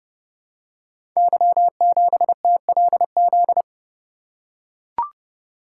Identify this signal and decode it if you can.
Morse code, heard as a keyed tone, about 30 wpm, 700 Hz. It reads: Y7TLZ
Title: Code Groups - 5 Letter:Numbers - 30wpm